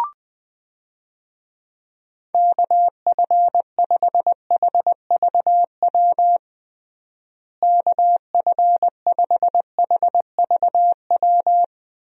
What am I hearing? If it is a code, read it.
Morse code, heard as a keyed tone, about 20 wpm, 700 Hz. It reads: KF5HVW KF5HVW
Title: USA Call Signs - 6 Characters - Repeated 3x - 20wpm